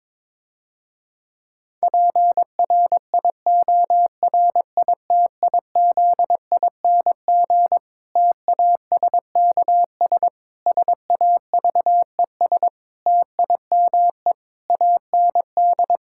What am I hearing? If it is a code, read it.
Morse code, heard as a keyed tone, about 22 wpm, 700 Hz. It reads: PRIORITIZING TASKS SAVES TIME AND
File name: Difficult Sentences with Hints - 22wpm